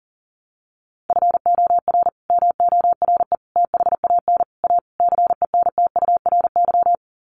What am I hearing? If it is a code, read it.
Morse code, heard as a keyed tone, about 40 wpm, 700 Hz. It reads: FOR MORE THAN A CENTURY
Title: Sets of 5 Words Encore2 40wpm